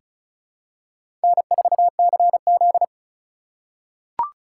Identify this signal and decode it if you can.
Morse code, heard as a keyed tone, about 36 wpm, 700 Hz. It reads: N4CZ
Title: USA Call Signs - 3x Character Spacing - 35wpm